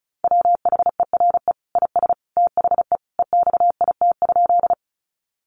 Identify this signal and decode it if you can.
Morse code, heard as a keyed tone, about 34 wpm, 700 Hz. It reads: WHERE IS THE EXIT?
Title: Sentences from Top 1100 Words 35wpm